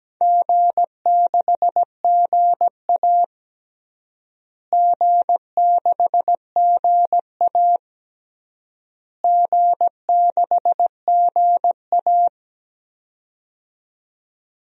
Morse code, heard as a keyed tone, about 17 wpm, 700 Hz. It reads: G6GA G6GA G6GA
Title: DXCC-100 Call Signs - 4 Characters - Repeated 3x - 17wpm